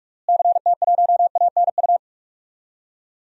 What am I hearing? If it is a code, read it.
Morse code, heard as a keyed tone, about 44 wpm, 700 Hz. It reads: CT1ANU
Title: DXCC-100 Call Signs - 6 Characters - Repeated 3x - 45wpm